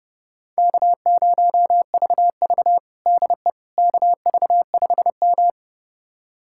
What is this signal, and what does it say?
Morse code, heard as a keyed tone, about 32 wpm, 700 Hz. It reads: K0VV DE KV5M
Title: USA Callsign DE Callsign - 4 Characters - 30wpm